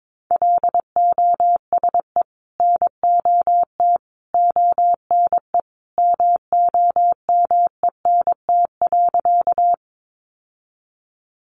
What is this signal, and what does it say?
Morse code, heard as a keyed tone, about 22 wpm, 700 Hz. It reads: LOSE NOT ONE MOMENT.
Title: Sentences from Top 1100 Words 22wpm